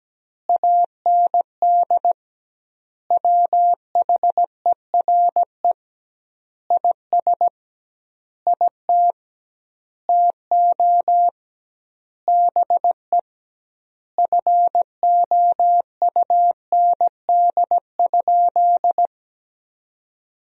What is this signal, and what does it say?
Morse code, heard as a keyed tone, about 17 wpm, 700 Hz. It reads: AND WHERE IS IT TO BE FOUND?
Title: 2x Word-Spacing - Sentences from Top 300 Words - 17wpm